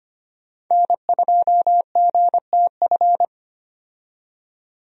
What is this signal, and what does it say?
Morse code, heard as a keyed tone, about 25 wpm, 700 Hz. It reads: N2GTF
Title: USA Call Signs - 5 Characters - 25wpm